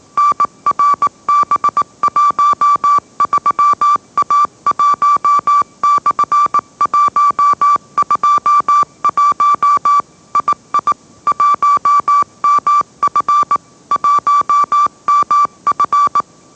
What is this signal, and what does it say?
Morse code, heard as a keyed tone, about 18 wpm, 1160 Hz. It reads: NRB13A1/121 II 1MF 1MF
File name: mrx / mrx081 20wpm